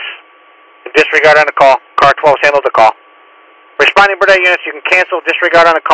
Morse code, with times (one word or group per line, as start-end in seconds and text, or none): none